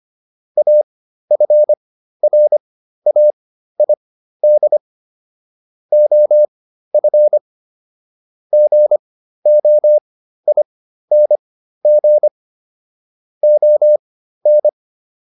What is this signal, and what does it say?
Morse code, heard as a keyed tone, about 25 wpm, spaced as for 13 wpm, 600 Hz. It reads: AFRAID OF GOING ON